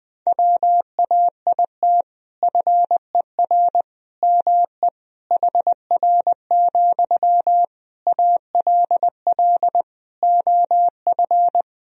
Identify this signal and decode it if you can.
Morse code, heard as a keyed tone, about 20 wpm, 700 Hz. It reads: WAIT FER ME HR, ALL OF
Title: Sentences from Top 500 Words and Abbreviations 20wpm